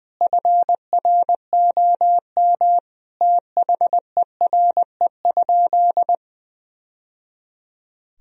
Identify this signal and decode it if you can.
Morse code, heard as a keyed tone, about 19 wpm, 700 Hz. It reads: FROM THERE?
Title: Sentences from Top 1000 Words - Review Part 1 - 20wpm